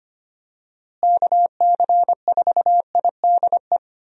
Morse code, heard as a keyed tone, about 25 wpm, 700 Hz. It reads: KC4IDE
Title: USA Call Signs - 6 Characters - Repeated 2x - 25wpm